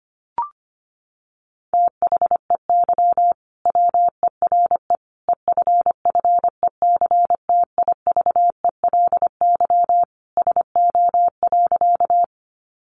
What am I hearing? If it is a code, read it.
Morse code, heard as a keyed tone, about 25 wpm, 700 Hz. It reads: THEY WERE EFFECTIVELY SO.
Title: Sentences from Top 1400 Words 25wpm